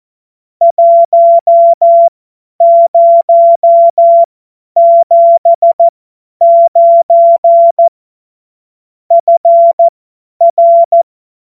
Morse code, heard as a keyed tone, about 14 wpm, spaced as for 10 wpm, 680 Hz. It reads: 1079 FR
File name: Morsekurs / Witze / Witz 08. August 2023 50 F70